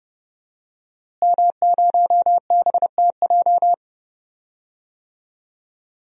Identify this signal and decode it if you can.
Morse code, heard as a keyed tone, about 31 wpm, 700 Hz. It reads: M0BTJ
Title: DXCC-100 Call Signs - 5 Characters - Repeated 3x - 30wpm